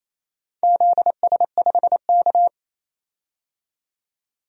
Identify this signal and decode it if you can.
Morse code, heard as a keyed tone, about 28 wpm, 700 Hz. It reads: ZS5K
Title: DXCC-100 Call Signs - 4 Characters - 28wpm